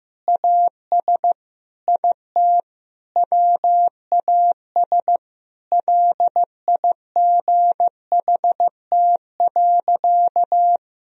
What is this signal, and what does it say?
Morse code, heard as a keyed tone, about 15 wpm, 700 Hz. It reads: AS IT WAS LIGHT.
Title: Sentences from Top 1000 Words - Review Part 1 - 15wpm